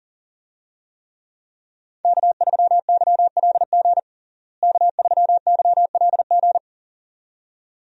Morse code, heard as a keyed tone, about 40 wpm, 700 Hz. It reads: K3YLG K3YLG
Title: USA Call Signs - 5 Characters - Repeated 2x - 40wpm